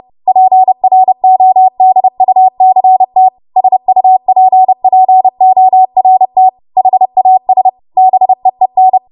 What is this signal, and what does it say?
Morse code, heard as a keyed tone, about 30 wpm, 750 Hz. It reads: PRODUCT SUPPORT HAS BEEN
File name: Morse / 30WPM